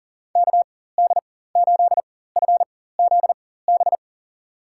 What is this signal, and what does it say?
Morse code, heard as a keyed tone, about 40 wpm, spaced as for 19 wpm, 700 Hz. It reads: KD8FZB
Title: USA Call Signs - 6 Characters - 4x Character Spacing - 40wpm